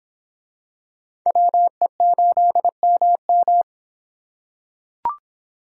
Morse code, heard as a keyed tone, about 26 wpm, 700 Hz. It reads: WE8MM